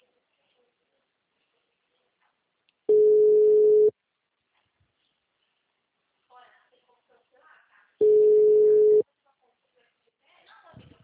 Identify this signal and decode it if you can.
Ringback tone (Europe, 425 Hz), with speech.